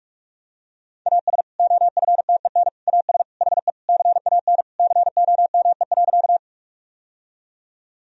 Morse code, heard as a keyed tone, about 46 wpm, 700 Hz. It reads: AS OFTEN AS HE CAN COME.
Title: Sentences from Top 1000 Words - Review Part 1 - 45wpm